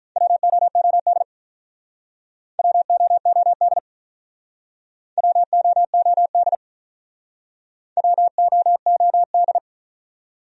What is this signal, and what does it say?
Morse code, heard as a keyed tone, about 46 wpm, 700 Hz. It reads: WOOD WOOD WOOD WOOD